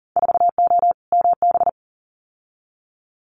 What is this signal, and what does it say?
Morse code, heard as a keyed tone, about 40 wpm, 700 Hz. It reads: 4O MB
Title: Winter Field Day Exchanges Encore 40wpm